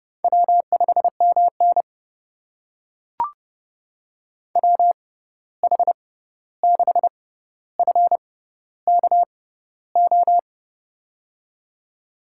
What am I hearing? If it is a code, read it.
Morse code, heard as a keyed tone, about 31 wpm, spaced as for 14 wpm, 700 Hz. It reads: W5MN WH6FKO
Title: USA Call Signs - 6x Character Spacing - 30wpm